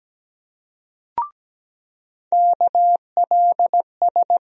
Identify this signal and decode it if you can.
Morse code, heard as a keyed tone, about 17 wpm, 700 Hz. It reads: KLS
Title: USA Call Sign Postfixes - 3 Letters 17wpm